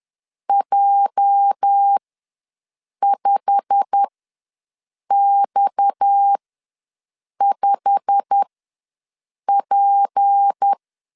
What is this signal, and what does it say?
Morse code, heard as a keyed tone, about 10 wpm, spaced as for 6 wpm, 800 Hz. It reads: J5X5P
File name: LEZIONI / 8 YJ45 / 8 YJ45 ws 10wpm 3843